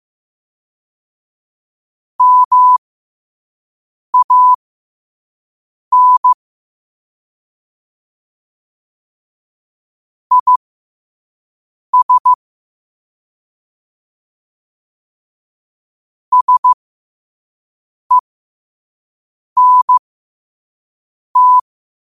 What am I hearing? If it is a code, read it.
Morse code, heard as a keyed tone, about 15 wpm, spaced as for 5 wpm, 1000 Hz. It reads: MAN IS SENT
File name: qotdfiles / 5wpm